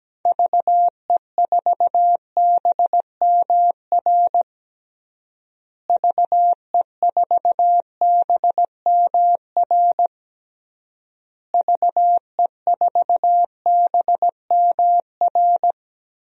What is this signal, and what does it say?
Morse code, heard as a keyed tone, about 17 wpm, 700 Hz. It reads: VE4BMR VE4BMR VE4BMR
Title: DXCC-100 Call Signs - 6 Characters - Repeated 3x - 17wpm